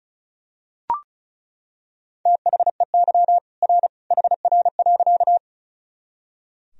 Morse code, heard as a keyed tone, about 36 wpm, 700 Hz. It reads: THEY R HR.
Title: Sentences from Top 200 Words and Abbreviations 35wpm